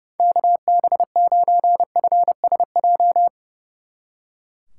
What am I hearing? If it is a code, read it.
Morse code, heard as a keyed tone, about 37 wpm, 700 Hz. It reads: KB9FSJ
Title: Speed Racing - USA 2x3 Call Signs 30wpm